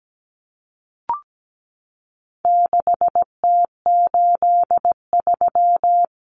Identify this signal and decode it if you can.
Morse code, heard as a keyed tone, about 17 wpm, 700 Hz. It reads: 6T83
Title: Common Cut Numbers - 4-Digit 17wpm